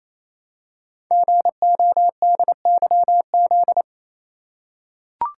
Morse code, heard as a keyed tone, about 29 wpm, 700 Hz. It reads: GODYZ
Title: Code Groups - 5 Letters - 28wpm